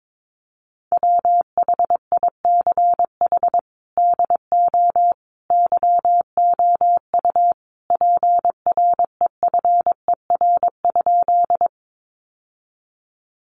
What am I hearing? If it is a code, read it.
Morse code, heard as a keyed tone, about 22 wpm, 700 Hz. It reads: WHICH DO YOU PREFER?
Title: Sentences from Top 1100 Words 22wpm